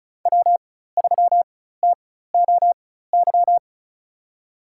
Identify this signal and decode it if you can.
Morse code, heard as a keyed tone, about 36 wpm, spaced as for 16 wpm, 700 Hz. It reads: W3TOY
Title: USA Call Signs - 5 Characters - 4x Character Spacing - 35wpm